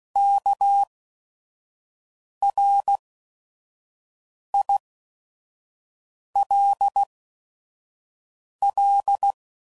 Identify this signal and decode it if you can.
Morse code, heard as a keyed tone, about 16 wpm, spaced as for 5 wpm, 800 Hz. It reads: KRILL
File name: Prefixes / By First Letter / Ks